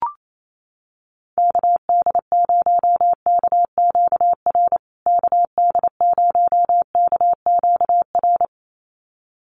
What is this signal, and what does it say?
Morse code, heard as a keyed tone, about 28 wpm, 700 Hz. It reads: KD0KQR KD0KQR
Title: USA Call Signs Repeated 2x 28wpm